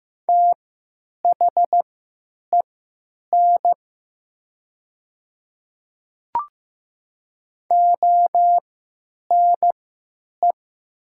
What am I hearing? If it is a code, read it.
Morse code, heard as a keyed tone, about 15 wpm, spaced as for 9 wpm, 700 Hz. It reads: THEN ONE
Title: 3x Character Spacing - Top 100 Words - 15wpm